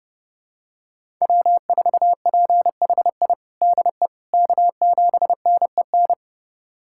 Morse code, heard as a keyed tone, about 30 wpm, 700 Hz. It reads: W4PHI DE K7NEN
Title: USA Callsign DE Callsign - 5 Characters - 30wpm